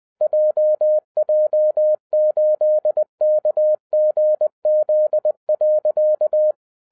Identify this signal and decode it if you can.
Morse code, heard as a keyed tone, about 20 wpm, 610 Hz. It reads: JJ8KGZ.
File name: 2012-01-mp3 / n16-THROUGH EIGHT MONTHS-20wpm